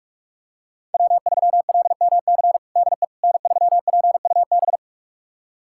Morse code, heard as a keyed tone, about 47 wpm, 700 Hz. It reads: W3LMC DE N3PUB
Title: USA Callsign DE Callsign - 5 Characters - 45wpm